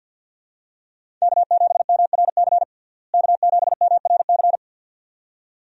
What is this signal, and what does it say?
Morse code, heard as a keyed tone, about 50 wpm, 700 Hz. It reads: K7MRC K7MRC
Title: USA Call Signs - 5 Characters - Repeated 2x - 50wpm